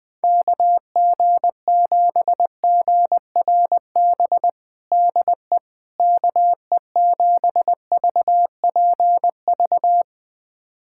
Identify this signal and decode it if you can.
Morse code, heard as a keyed tone, about 20 wpm, 700 Hz. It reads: KG7GRB DE KE7VPV
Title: USA Callsign DE Callsign - 6 Characters - 20wpm